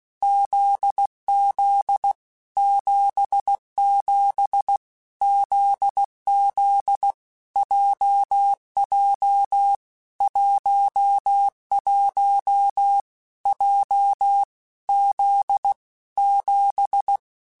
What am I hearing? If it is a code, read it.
Morse code, heard as a keyed tone, about 16 wpm, 800 Hz. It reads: ZZ 77 ZZ JJ 11 J Z 7
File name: Basic Letter Groups By7Groups / JZ17 / JZ17@15